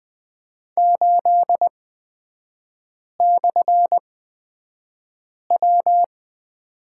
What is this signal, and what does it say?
Morse code, heard as a keyed tone, about 20 wpm, spaced as for 5 wpm, 690 Hz. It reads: 8/W
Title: 229 Lesson 37e - Letter x - Three Characters 20wpm